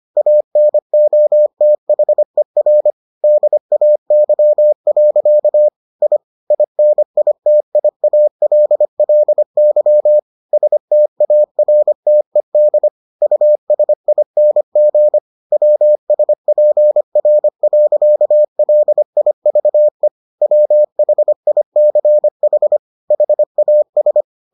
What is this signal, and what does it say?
Morse code, heard as a keyed tone, about 25 wpm, 600 Hz. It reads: ANOTHER DAY. I INITIALLY STARTED USING WSPR.LIVE WHICH HAS